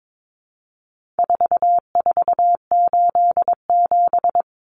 Morse code, heard as a keyed tone, about 22 wpm, 700 Hz. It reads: 4487
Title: Numbers - 4-Digit 22wpm